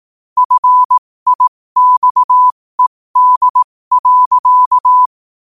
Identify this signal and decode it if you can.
Morse code, heard as a keyed tone, about 18 wpm, spaced as for 15 wpm, 1000 Hz. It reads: FIXED.
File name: qotdfiles / 15wpm